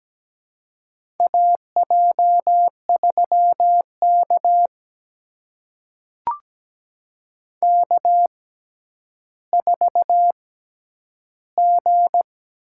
Morse code, heard as a keyed tone, about 17 wpm, spaced as for 10 wpm, 700 Hz. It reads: AJ3K K4G
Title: USA Call Signs - 6x Character Spacing - 17wpm